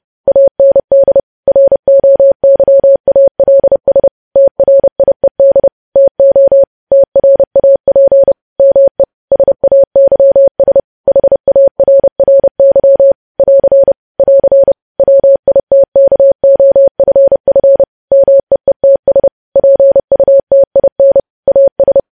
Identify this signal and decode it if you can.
Morse code, heard as a keyed tone, about 30 wpm, 550 Hz. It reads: AND ROYALS TRIED TO TRAP ME SAYS HARRY <AR> <AR> WITKOFF MEETS PUTIN AS
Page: news headlines in morse code at 30 wpm